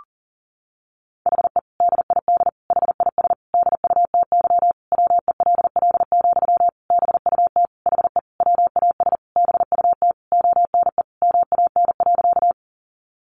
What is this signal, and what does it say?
Morse code, heard as a keyed tone, about 40 wpm, 700 Hz. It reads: HE DID HIS DUTY WELL, BUT HE WAS BUT ONE MAN.
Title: Sentences from Top 1500 Words 40wpm